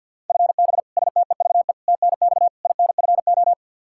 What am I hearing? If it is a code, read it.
Morse code, heard as a keyed tone, about 49 wpm, 700 Hz. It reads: FB STEVE TNX INFO
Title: Practice QSO - Exchange 3 - More Info - 50wpm